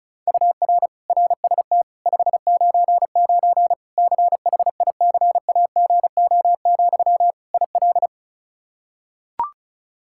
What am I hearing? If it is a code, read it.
Morse code, heard as a keyed tone, about 34 wpm, 700 Hz. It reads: UR RST 599 CHICAGO, IL
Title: Practice QSO - Exchange 1 - Beginning - 35wpm